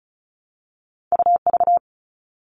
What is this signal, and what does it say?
Morse code, heard as a keyed tone, about 36 wpm, 690 Hz. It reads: UV
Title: All Cut Numbers - 2-Digit 35wpm